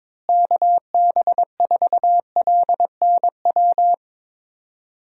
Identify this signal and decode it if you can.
Morse code, heard as a keyed tone, about 22 wpm, 700 Hz. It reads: KB4LNW
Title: USA Call Signs - 6 Characters - Repeated 2x - 22wpm